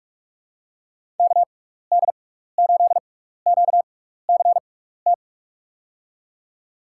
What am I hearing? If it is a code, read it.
Morse code, heard as a keyed tone, about 45 wpm, spaced as for 15 wpm, 700 Hz. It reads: KD8QCT
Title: USA Call Signs - 6 Characters - 6x Character Spacing - 45wpm